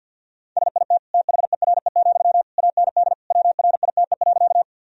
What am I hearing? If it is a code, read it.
Morse code, heard as a keyed tone, about 48 wpm, 700 Hz. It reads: SIT THERE, AND WRITE.
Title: Sentences from Top 1000 Words - Review Part 1 - 50wpm